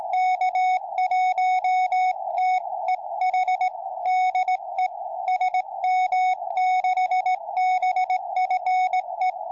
Morse code, heard as a keyed tone, about 17 wpm, 740 Hz. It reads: K1TEH DE SM6BFE